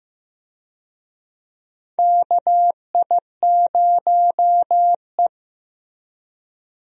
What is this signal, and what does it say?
Morse code, heard as a keyed tone, about 15 wpm, 700 Hz. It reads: KI0E
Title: USA Call Signs - 4 Characters - Repeated 3x - 15wpm